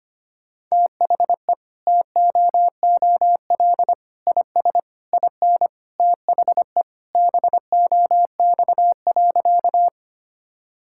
Morse code, heard as a keyed tone, about 25 wpm, 700 Hz. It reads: THE TOOL IS IN THE BOX.
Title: Sentences from Top 1000 Words - Review Part 6 - 25wpm